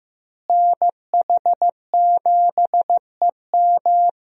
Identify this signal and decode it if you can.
Morse code, heard as a keyed tone, about 15 wpm, 700 Hz. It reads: NH7EM
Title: USA Call Signs - 5 Characters - Repeated 3x - 15wpm